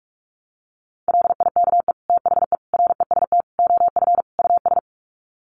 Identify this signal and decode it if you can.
Morse code, heard as a keyed tone, about 44 wpm, 700 Hz. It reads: LIKE THE REST OF US
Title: Sets of 5 Words Encore2 45wpm